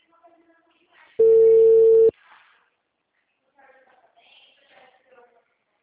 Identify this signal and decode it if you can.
Ringback tone (China, 450 Hz), with speech.
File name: out- -2022- - - .40723